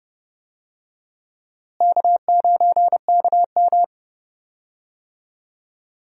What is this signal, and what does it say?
Morse code, heard as a keyed tone, about 31 wpm, 700 Hz. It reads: K9KM
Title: USA Call Signs - 4 Characters - Repeated 3x - 30wpm